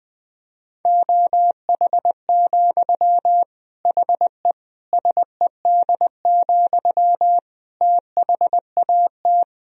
Morse code, heard as a keyed tone, about 21 wpm, 700 Hz. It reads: OH, HE SED, THAT
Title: Sentences from Top 500 Words and Abbreviations 20wpm